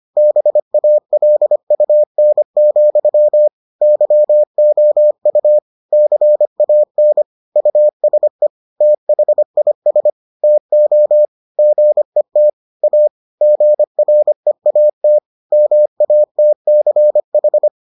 Morse code, heard as a keyed tone, about 25 wpm, 600 Hz. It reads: BALUN, YOU CAN USE THIS TO GET A GREAT MATCH